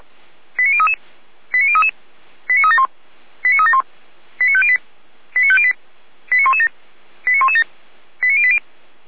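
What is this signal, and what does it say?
Five-tone selective call (ZVEI), heard as quick runs of five tones, tones of about 70 ms. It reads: 89033 89033 89371 89371 89508 89508 89117 89117 89080
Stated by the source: ZVEI five-tone selective call received by SDR.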